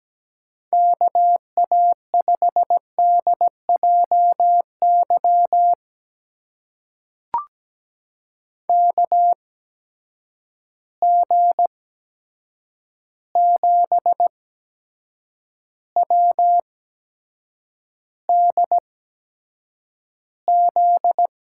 Morse code, heard as a keyed tone, about 17 wpm, spaced as for 7 wpm, 700 Hz. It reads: KA5DJY KG7WDZ
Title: USA Call Signs - 6 Characters - 8x Character Spacing - 17wpm